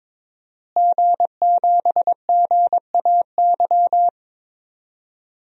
Morse code, heard as a keyed tone, about 23 wpm, 700 Hz. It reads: G7GAY
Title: DXCC-100 Call Signs - 5 Characters - Repeated 3x - 22wpm